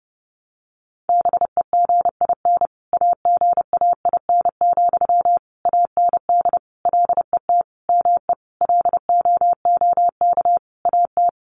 Morse code, heard as a keyed tone, about 30 wpm, 700 Hz. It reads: BEGIN AGAIN, AND LET ME LOOK AT
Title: Sentences from Top 800 Words 30wpm